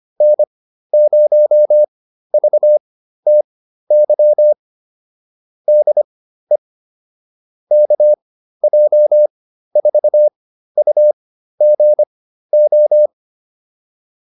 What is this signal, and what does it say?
Morse code, heard as a keyed tone, about 25 wpm, spaced as for 13 wpm, 600 Hz. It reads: N0VTY DE KJ4UGO